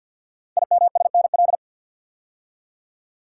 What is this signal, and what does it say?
Morse code, heard as a keyed tone, about 50 wpm, 690 Hz. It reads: IMSNL
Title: Code Groups - 5 Letters - 50wpm